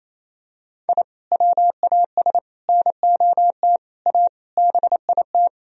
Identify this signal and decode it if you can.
Morse code, heard as a keyed tone, about 28 wpm, 700 Hz. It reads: I WAS NOT A BIT
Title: Sentences from Top 1000 Words - Review Part 4 - 28wpm